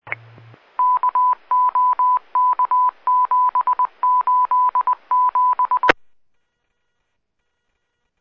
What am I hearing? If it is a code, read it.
Morse code, heard as a keyed tone, about 20 wpm, 1000 Hz. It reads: KOK787
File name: 1 Fire Dept / 3646 07-40 1 Fire Dept 154.3700 RMFD F-1 799 Rural-Metro Fire 6-17-2018